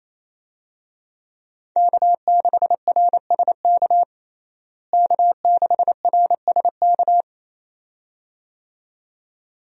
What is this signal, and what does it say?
Morse code, heard as a keyed tone, about 28 wpm, 700 Hz. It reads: K6RSK K6RSK
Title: USA Call Signs - 5 Characters - Repeated 2x - 28wpm